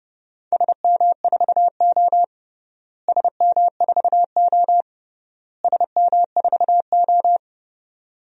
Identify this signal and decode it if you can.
Morse code, heard as a keyed tone, about 28 wpm, 700 Hz. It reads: SM4O SM4O SM4O
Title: DXCC-100 Call Signs - 4 Characters - Repeated 3x - 30wpm